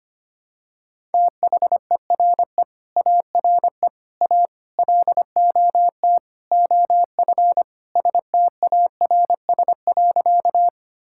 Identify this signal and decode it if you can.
Morse code, heard as a keyed tone, about 25 wpm, 700 Hz. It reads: THERE ARE A LOT OF STARS.
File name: practice / 25wpm / Sentences with 4-Word Phrases - 25wpm